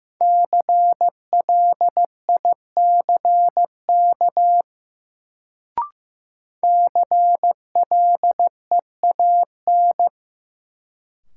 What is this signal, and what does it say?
Morse code, heard as a keyed tone, about 15 wpm, 700 Hz. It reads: CLICK CLEAN
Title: Sentences from Top 1100 Words - New Word Drill - 15wpm